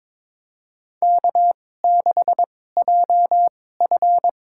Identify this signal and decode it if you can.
Morse code, heard as a keyed tone, about 22 wpm, spaced as for 16 wpm, 700 Hz. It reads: K6JF
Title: USA Call Signs - 4 Characters - 2x Character Spacing - 22wpm